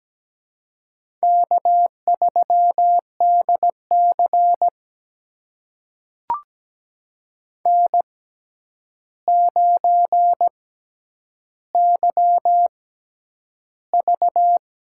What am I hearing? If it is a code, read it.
Morse code, heard as a keyed tone, about 17 wpm, spaced as for 9 wpm, 700 Hz. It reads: K3DC N9YV
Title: USA Call Signs - 4 Characters - 6x Character Spacing - 17wpm